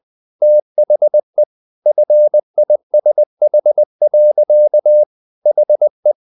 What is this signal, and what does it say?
Morse code, heard as a keyed tone, about 20 wpm, 600 Hz. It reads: THE FISH. HE
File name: the old man and the sea part 2 20wpm